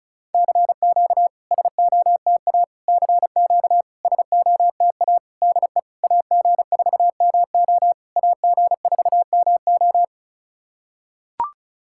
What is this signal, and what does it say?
Morse code, heard as a keyed tone, about 36 wpm, 700 Hz. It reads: CQ SOTA CQ SOTA DE AG4MO AG4MO
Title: SOTA QSO Practice 35wpm